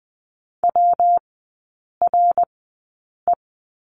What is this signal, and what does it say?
Morse code, heard as a keyed tone, about 20 wpm, spaced as for 8 wpm, 700 Hz. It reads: WRE
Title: 127 Lesson 21e - Number 6 - Three Characters 20wpm